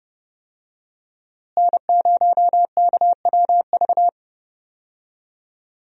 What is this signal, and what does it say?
Morse code, heard as a keyed tone, about 32 wpm, 700 Hz. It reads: N0KWV
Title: USA Call Signs - 5 Characters - Repeated 3x - 30wpm